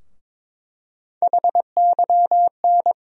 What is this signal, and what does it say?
Morse code, heard as a keyed tone, about 22 wpm, 700 Hz. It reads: HYN
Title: USA Call Sign Postfixes - 3 Letters 22wpm